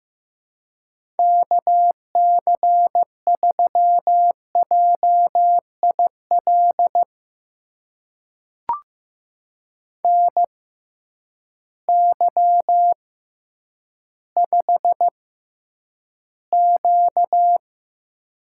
Morse code, heard as a keyed tone, about 15 wpm, spaced as for 9 wpm, 700 Hz. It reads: KC3JIL NY5Q
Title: USA Call Signs - 6x Character Spacing - 15wpm